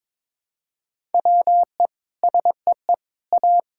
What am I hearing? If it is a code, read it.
Morse code, heard as a keyed tone, about 22 wpm, 700 Hz. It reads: WE SEE A
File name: Sentences with 9-Word Phrases - 22wpm